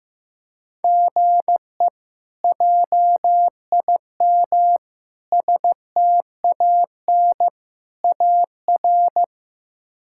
Morse code, heard as a keyed tone, about 15 wpm, 700 Hz. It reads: GE JIM STAN AR
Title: K1USN SST QSO Practice - 15wpm